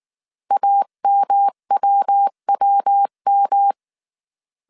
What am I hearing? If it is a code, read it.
Morse code, heard as a keyed tone, about 19 wpm, 800 Hz. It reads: AMWWM
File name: CW / DET / 1 3 1500C S B 18wpm 1916